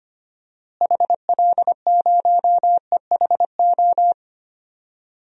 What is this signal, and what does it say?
Morse code, heard as a keyed tone, about 25 wpm, 700 Hz. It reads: HL0EHO
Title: DXCC-100 Call Signs - 6 Characters - Repeated 3x - 25wpm